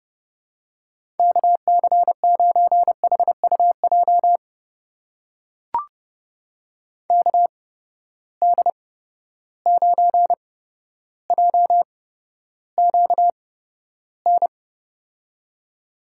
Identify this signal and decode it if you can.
Morse code, heard as a keyed tone, about 30 wpm, spaced as for 13 wpm, 700 Hz. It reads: KC9HUJ KD9JQN
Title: USA Call Signs - 6 Characters - 8x Character Spacing - 30wpm